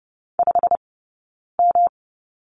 Morse code, heard as a keyed tone, about 30 wpm, 700 Hz. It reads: 5 M
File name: Lesson 20d - Number 3 - Two Characters 30wpm